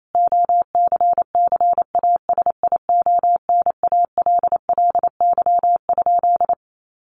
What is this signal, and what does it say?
Morse code, heard as a keyed tone, about 28 wpm, 700 Hz. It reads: OCCASIONALLY?
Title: Sentences from Top 1300 Words 28wpm